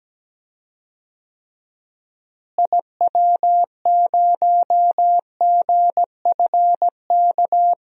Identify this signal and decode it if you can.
Morse code, heard as a keyed tone, about 17 wpm, 700 Hz. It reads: IW0GFK
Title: DXCC-100 Call Signs - 6 Characters - Repeated 2x - 17wpm